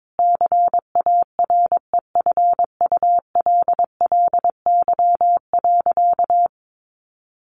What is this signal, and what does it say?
Morse code, heard as a keyed tone, about 22 wpm, 700 Hz. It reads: CAREFULLY.
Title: Sentences from Top 1300 Words 22wpm